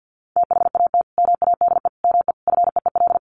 Morse code, heard as a keyed tone, about 50 wpm, 700 Hz. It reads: THAT MADE ME FEEL
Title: Sentences from Top 1000 Words 50wpm